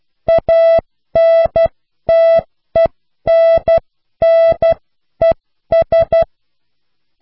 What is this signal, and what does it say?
Morse code, heard as a keyed tone, about 12 wpm, spaced as for 10 wpm, 660 Hz. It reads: ANTENNES